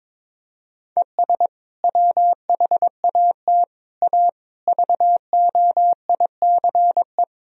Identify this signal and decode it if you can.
Morse code, heard as a keyed tone, about 22 wpm, 700 Hz. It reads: ES WHAT A VOICE
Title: Sentences from Top 500 Words and Abbreviations 22wpm